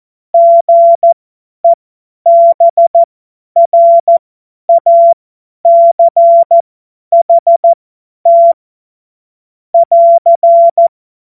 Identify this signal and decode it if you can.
Morse code, heard as a keyed tone, about 14 wpm, spaced as for 10 wpm, 680 Hz. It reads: GEBRACHT <AR>
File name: Morsekurs / Witze / Witz 02. Mai 2023 50 F70